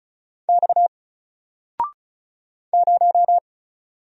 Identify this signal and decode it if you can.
Morse code, heard as a keyed tone, about 36 wpm, 700 Hz. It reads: X 0
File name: Single-Letter-Number-35wpm